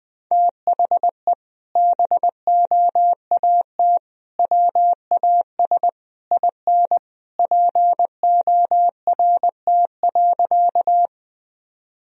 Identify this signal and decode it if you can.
Morse code, heard as a keyed tone, about 20 wpm, 700 Hz. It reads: THE BOAT WAS IN PORT.
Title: Sentences from Top 1000 Words - Review Part 1 - 20wpm